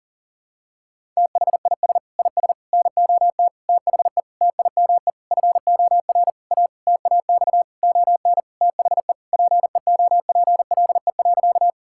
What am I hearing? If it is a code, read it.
Morse code, heard as a keyed tone, about 40 wpm, 700 Hz. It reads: THIS IS NOT THE TIME FOR A TAX ON THE PEOPLE.
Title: Sentences from Top 1000 Words - Review Part 5 - 40wpm